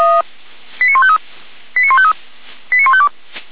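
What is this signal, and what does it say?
Five-tone selective call (ZVEI), heard as quick runs of five tones, tones of about 70 ms. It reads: 89153 89153 89152
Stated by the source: ZVEI five-tone call 89060 received by SDR, logged by Feuerwehr Orlamünde.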